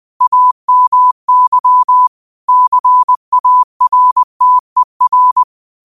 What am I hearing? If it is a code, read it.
Morse code, heard as a keyed tone, about 20 wpm, 1000 Hz. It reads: AMY CARTER
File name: qotdfiles / 20wpm